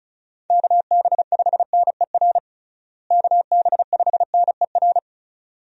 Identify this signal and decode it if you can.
Morse code, heard as a keyed tone, about 36 wpm, 700 Hz. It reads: KB5NER KB5NER
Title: USA Call Signs - 6 Characters - Repeated 2x - 35wpm